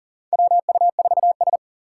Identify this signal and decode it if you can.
Morse code, heard as a keyed tone, about 37 wpm, 700 Hz. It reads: WU4S
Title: USA Call Signs - 4x Character Spacing - 40wpm